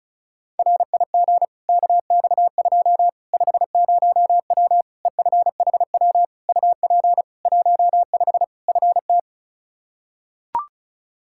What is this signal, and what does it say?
Morse code, heard as a keyed tone, about 36 wpm, 700 Hz. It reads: RIG KX2 50W EFHW UP 15 FT
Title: K1USN SST QSO Practice 35wpm